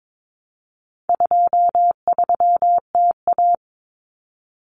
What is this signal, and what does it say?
Morse code, heard as a keyed tone, about 22 wpm, 700 Hz. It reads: 23TA
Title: Common Cut Numbers - 4-Digit 22wpm